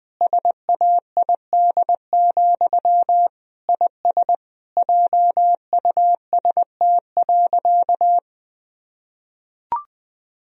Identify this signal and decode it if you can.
Morse code, heard as a keyed tone, about 20 wpm, 700 Hz. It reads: SAID, IS JUST.
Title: Sentences from Top 1000 Words - Review Part 4 - 20wpm